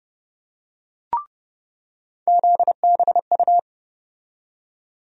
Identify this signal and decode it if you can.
Morse code, heard as a keyed tone, about 30 wpm, 700 Hz. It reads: ZBU
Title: USA Call Sign Postfixes - 3 Letters 30wpm